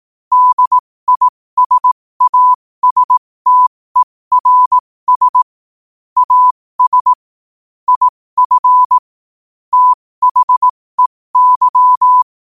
Morse code, heard as a keyed tone, about 18 wpm, spaced as for 15 wpm, 1000 Hz. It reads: DISASTERS AS IF THEY
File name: qotdfiles / 15wpm